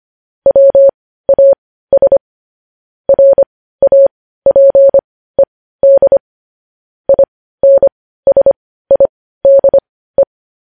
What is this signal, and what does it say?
Morse code, heard as a keyed tone, about 25 wpm, spaced as for 15 wpm, 550 Hz. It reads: WAS RAPED INSIDE